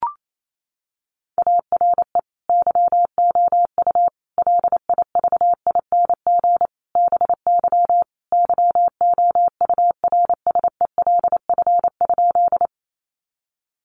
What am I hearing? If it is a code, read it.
Morse code, heard as a keyed tone, about 28 wpm, 700 Hz. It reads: ARE YOU LIVING BY YOURSELF?
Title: Sentences from Top 1100 Words 28wpm